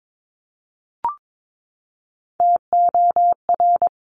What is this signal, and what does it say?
Morse code, heard as a keyed tone, about 22 wpm, 690 Hz. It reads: TOR
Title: Top Three Letter Combinations in Words 22wpm